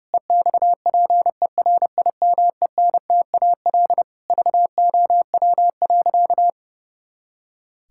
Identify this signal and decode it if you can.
Morse code, heard as a keyed tone, about 30 wpm, 700 Hz. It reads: EXPERIMENTAL VOW.
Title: Pangram Challenge v2 - 30wpm